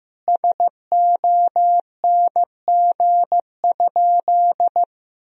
Morse code, heard as a keyed tone, about 15 wpm, 700 Hz. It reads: SONG?
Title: Sentences from Top 1000 Words - Review Part 2 - 15wpm